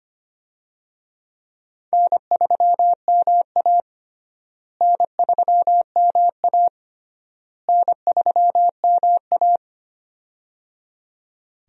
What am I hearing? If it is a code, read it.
Morse code, heard as a keyed tone, about 25 wpm, 700 Hz. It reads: N3MA N3MA N3MA
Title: USA Call Signs - 4 Characters - Repeated 3x - 25wpm